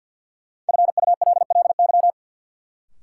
Morse code, heard as a keyed tone, about 48 wpm, 700 Hz. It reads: FULLY
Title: Sentences from Top 1300 Words - New Word Drill - 50wpm